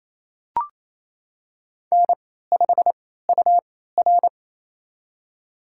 Morse code, heard as a keyed tone, about 28 wpm, spaced as for 16 wpm, 700 Hz. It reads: N5UR
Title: USA Call Signs - 4 Characters - 3x Character Spacing - 28wpm